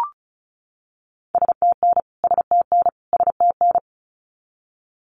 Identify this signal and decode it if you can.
Morse code, heard as a keyed tone, about 34 wpm, 700 Hz. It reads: STN STN STN
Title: Abbreviations Repeated 3x - 35wpm